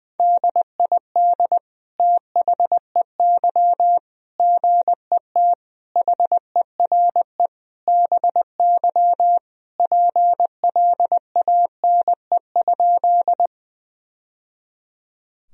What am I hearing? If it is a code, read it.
Morse code, heard as a keyed tone, about 20 wpm, 700 Hz. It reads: DID THEY GET HERE BY PLANE?
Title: Sentences from Top 1000 Words - Review Part 3 - 20wpm